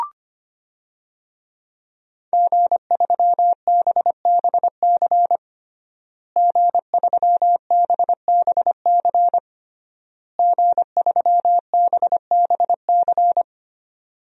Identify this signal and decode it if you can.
Morse code, heard as a keyed tone, about 25 wpm, 700 Hz. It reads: G3BBC G3BBC G3BBC
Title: DXCC-100 Call Signs - 5 Characters - Repeated 3x - 25wpm